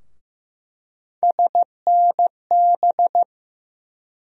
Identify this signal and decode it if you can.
Morse code, heard as a keyed tone, about 15 wpm, 700 Hz. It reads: SNB
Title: USA Call Sign Postfixes - 3 Letters 15wpm